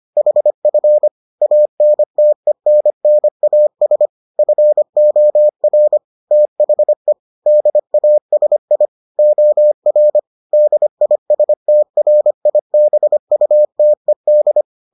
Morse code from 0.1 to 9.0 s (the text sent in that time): HF ANTENNAS FOR THE DASI